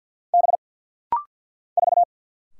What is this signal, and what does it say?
Morse code, heard as a keyed tone, about 50 wpm, 700 Hz. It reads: B 4
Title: Single Letter-Number - Rapid-Fire 50wpm